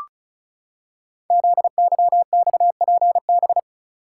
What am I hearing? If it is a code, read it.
Morse code, heard as a keyed tone, about 36 wpm, 700 Hz. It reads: ZYXPB
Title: Code Groups - 5 Letter:Numbers - 35wpm